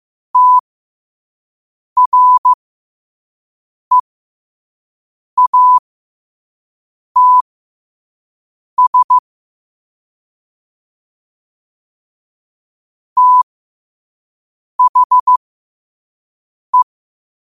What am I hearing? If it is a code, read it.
Morse code, heard as a keyed tone, about 15 wpm, spaced as for 5 wpm, 1000 Hz. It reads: TREATS THE
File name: qotdfiles / 5wpm